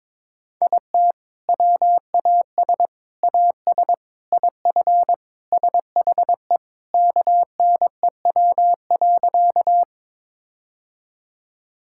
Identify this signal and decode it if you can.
Morse code, heard as a keyed tone, about 22 wpm, 700 Hz. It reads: IT WAS AS IF SHE KNEW.
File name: practice / 22wpm / Sentences with 4-Word Phrases - 22wpm